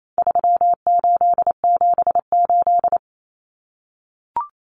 Morse code, heard as a keyed tone, about 28 wpm, 700 Hz. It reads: 3878
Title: Numbers - 4-Digit 28wpm